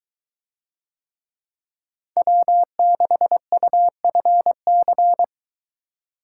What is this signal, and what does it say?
Morse code, heard as a keyed tone, about 23 wpm, 700 Hz. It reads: W6UFC